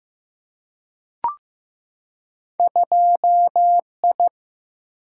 Morse code, heard as a keyed tone, about 15 wpm, 700 Hz. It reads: 2I
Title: Single-Letter Single-Number 15wpm